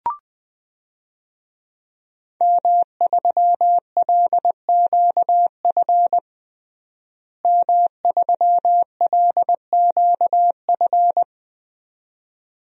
Morse code, heard as a keyed tone, about 20 wpm, 700 Hz. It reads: M3LQF M3LQF
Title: DXCC-100 Call Signs - 5 Characters - Repeated 2x - 20wpm